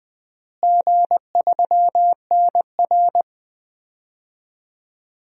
Morse code, heard as a keyed tone, about 20 wpm, 700 Hz. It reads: G3NR
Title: DXCC-100 Call Signs - 4 Characters - Repeated 3x - 20wpm